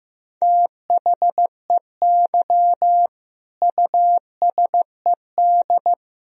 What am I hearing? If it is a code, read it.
Morse code, heard as a keyed tone, about 15 wpm, 700 Hz. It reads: THEY USED
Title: Sentences from Top 1000 Words - Review Part 4 - 15wpm